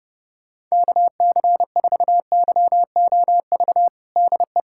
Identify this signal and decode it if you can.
Morse code, heard as a keyed tone, about 30 wpm, 700 Hz. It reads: KC4YOV DE
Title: USA Callsign DE Callsign - 6 Characters - 30wpm